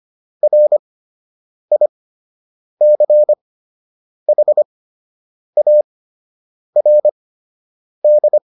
Morse code, heard as a keyed tone, about 25 wpm, spaced as for 8 wpm, 600 Hz. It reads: RICHARD